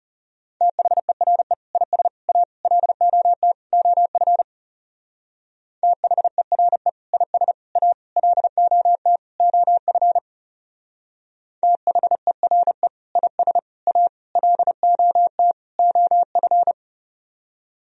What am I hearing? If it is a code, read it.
Morse code, heard as a keyed tone, about 38 wpm, 700 Hz. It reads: THERE IS A LOT OF THERE IS A LOT OF THERE IS A LOT OF